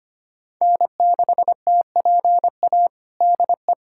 Morse code, heard as a keyed tone, about 25 wpm, 700 Hz. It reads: N6TPA DE
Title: USA Callsign DE Callsign - 5 Characters - 25wpm